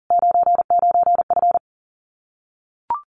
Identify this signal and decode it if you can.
Morse code, heard as a keyed tone, about 40 wpm, 700 Hz. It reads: 99F
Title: Common QSO Phrases Encore 40wpm